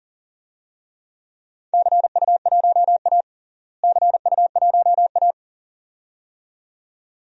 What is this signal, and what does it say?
Morse code, heard as a keyed tone, about 37 wpm, 700 Hz. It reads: CU1A CU1A
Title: DXCC-100 Call Signs - 4 Characters - Repeated 2x - 40wpm